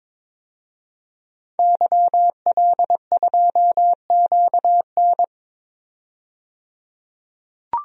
Morse code, heard as a keyed tone, about 22 wpm, 700 Hz. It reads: YL2QN
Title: DXCC-100 Call Signs - 5 Characters - 22wpm